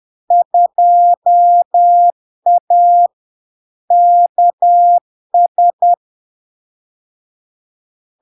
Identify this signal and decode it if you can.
Morse code, heard as a keyed tone, about 10 wpm, 700 Hz. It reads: 2A KS
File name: LARK-FD-2022-Log-in-CW-10wpm-700-Hz